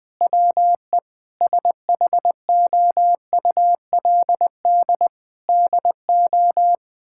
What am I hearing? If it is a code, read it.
Morse code, heard as a keyed tone, about 20 wpm, 700 Hz. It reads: WE SHOULD DO
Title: Sentences from Top 1600 Words 20wpm